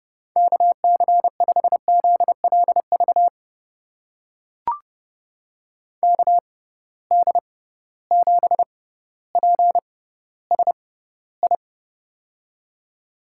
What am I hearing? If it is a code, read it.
Morse code, heard as a keyed tone, about 30 wpm, spaced as for 15 wpm, 700 Hz. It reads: KC5ZLV KD7PSI
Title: USA Call Signs - 6 Characters - 6x Character Spacing - 30wpm